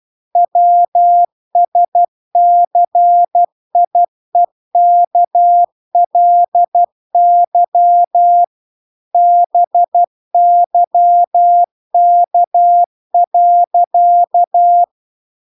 Morse code, heard as a keyed tone, about 12 wpm, 700 Hz. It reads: WSCIEKLY BYK.